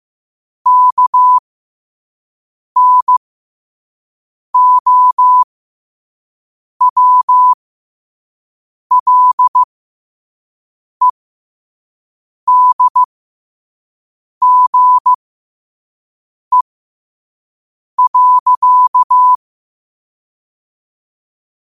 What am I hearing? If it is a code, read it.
Morse code, heard as a keyed tone, about 15 wpm, spaced as for 5 wpm, 1000 Hz. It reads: KNOWLEDGE.
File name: qotdfiles / 5wpm